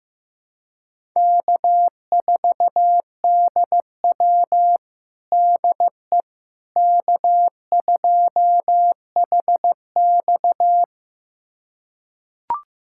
Morse code, heard as a keyed tone, about 15 wpm, 700 Hz. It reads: K4DW DE K2HX
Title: USA Callsign DE Callsign - 4 Characters - 15wpm